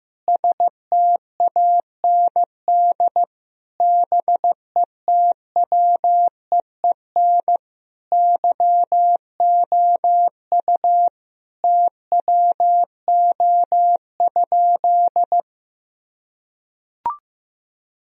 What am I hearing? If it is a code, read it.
Morse code, heard as a keyed tone, about 16 wpm, 700 Hz. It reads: STAND BETWEEN YOU TWO?
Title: Sentences from Top 1000 Words - Review Part 1 - 15wpm